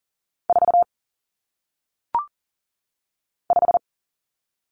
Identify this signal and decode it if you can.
Morse code, heard as a keyed tone, about 40 wpm, 700 Hz. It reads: 4 5
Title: Common Cut Numbers - 1-Digit 40wpm